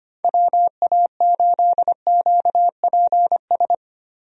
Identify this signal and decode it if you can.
Morse code, heard as a keyed tone, about 25 wpm, 700 Hz. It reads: WA8QPS
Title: USA Call Signs - 6 Characters - Repeated 3x - 25wpm